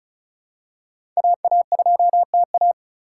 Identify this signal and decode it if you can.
Morse code, heard as a keyed tone, about 34 wpm, 700 Hz. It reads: AA2TA
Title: USA Call Signs - 5 Characters - 2x Character Spacing - 35wpm